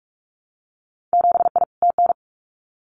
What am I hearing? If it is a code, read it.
Morse code, heard as a keyed tone, about 44 wpm, 700 Hz. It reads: 7I TN
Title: Winter Field Day Exchanges Encore 45wpm